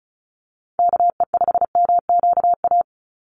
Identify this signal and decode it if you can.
Morse code, heard as a keyed tone, about 34 wpm, 700 Hz. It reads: KE5MQA
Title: USA Call Signs 35wpm